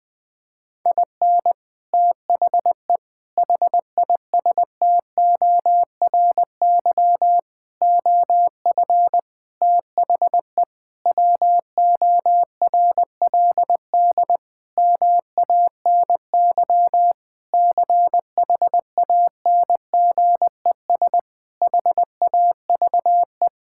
Morse code, heard as a keyed tone, about 20 wpm, 700 Hz. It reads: IN THE HISTORY OF THE WORLD MANY CHANGES HAVE
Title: Sentences with 6-Word Phrases - 20wpm